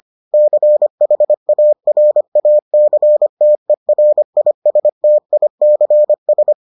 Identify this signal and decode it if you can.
Morse code, heard as a keyed tone, about 25 wpm, 600 Hz. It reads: CHARACTERISTICS